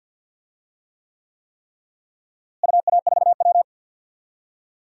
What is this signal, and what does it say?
Morse code, heard as a keyed tone, about 53 wpm, 700 Hz. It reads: UA4W
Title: DXCC-100 Call Signs - 4 Characters - Repeated 3x - 50wpm